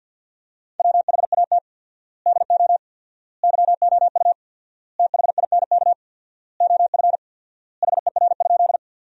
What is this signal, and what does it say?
Morse code, heard as a keyed tone, about 50 wpm, 700 Hz. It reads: WHAT DO YOU THINK OF HER?
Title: 4x Word-Spacing - Sentences from Top 200 Words - 50wpm